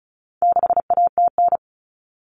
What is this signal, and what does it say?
Morse code, heard as a keyed tone, about 34 wpm, 700 Hz. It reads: 6ATN
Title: All Cut Numbers - 4-Digit 35wpm